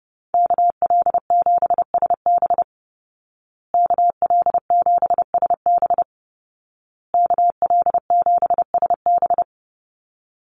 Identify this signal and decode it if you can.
Morse code, heard as a keyed tone, about 30 wpm, 700 Hz. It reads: KL7SB KL7SB KL7SB
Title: Call Signs Repeated 2x - 30wpm